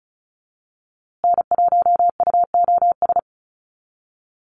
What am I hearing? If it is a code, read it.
Morse code, heard as a keyed tone, about 34 wpm, 700 Hz. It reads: N1UOS
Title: USA Call Signs Repeated 3x 35wpm